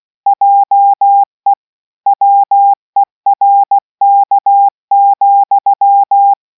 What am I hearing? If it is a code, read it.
Morse code, heard as a keyed tone, about 16 wpm, 800 Hz. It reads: JE WERK,